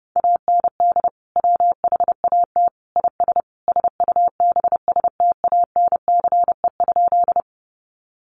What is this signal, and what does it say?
Morse code, heard as a keyed tone, about 30 wpm, 700 Hz. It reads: AND WHAT IS SUBSTANCE?
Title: Sentences from Top 1500 Words 30wpm